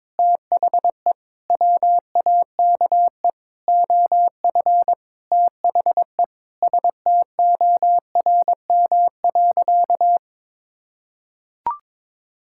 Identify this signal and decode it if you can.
Morse code, heard as a keyed tone, about 22 wpm, 700 Hz. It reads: THE WAKE OF THE STORM.
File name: practice / 22wpm / Sentences with 4-Word Phrases - 22wpm